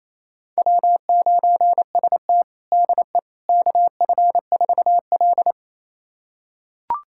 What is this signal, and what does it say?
Morse code, heard as a keyed tone, about 28 wpm, 700 Hz. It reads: W9ST DE KF4L
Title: USA Callsign DE Callsign - 4 Characters - 28wpm